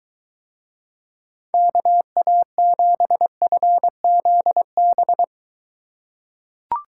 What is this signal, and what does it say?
Morse code, heard as a keyed tone, about 22 wpm, 700 Hz. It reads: KA7FZB